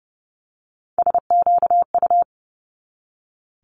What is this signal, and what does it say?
Morse code, heard as a keyed tone, about 30 wpm, 700 Hz. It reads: SQU
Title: Multi-letter Phonemes 30wpm